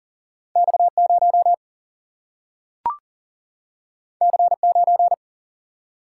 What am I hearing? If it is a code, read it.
Morse code, heard as a keyed tone, about 39 wpm, 700 Hz. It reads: X0 C9
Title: Single-Letter Single-Number 40wpm